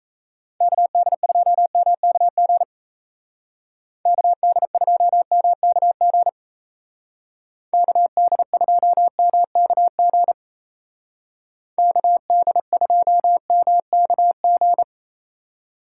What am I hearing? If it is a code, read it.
Morse code, heard as a keyed tone, about 37 wpm, 700 Hz. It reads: KD2MKG KD2MKG KD2MKG KD2MKG